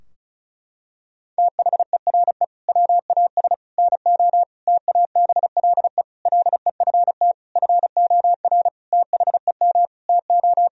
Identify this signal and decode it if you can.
Morse code, heard as a keyed tone, about 36 wpm, 700 Hz. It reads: THERE WAS NO TABLE LEFT FOR THEM TO
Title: Sentences from Top 1000 Words - Review Part 2 - 35wpm